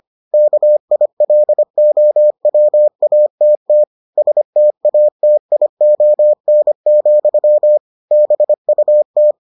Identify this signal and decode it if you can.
Morse code, heard as a keyed tone, about 25 wpm, 600 Hz. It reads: KILOWATT STATION, BUT